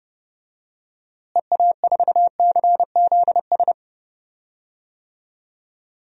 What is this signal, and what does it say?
Morse code, heard as a keyed tone, about 30 wpm, 700 Hz. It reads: EA4CZS
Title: DXCC-100 Call Signs - 6 Characters - Repeated 3x - 30wpm